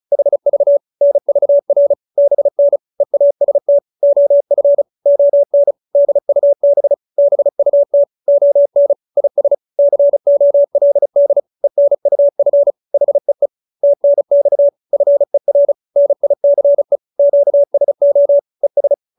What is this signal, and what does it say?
Morse code, heard as a keyed tone, about 34 wpm, 570 Hz. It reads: HV NVR BN EAST OF ON DUB BUT ON IS COLD ENUF HEE TNX FER NICE QSO ES
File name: QSO314 35